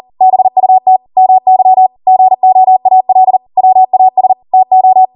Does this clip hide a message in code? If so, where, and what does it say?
Yes: Morse code from 0.2 to 5.0 s, heard as a keyed tone, about 40 wpm, 750 Hz. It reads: BUT MY GOAL WAS TO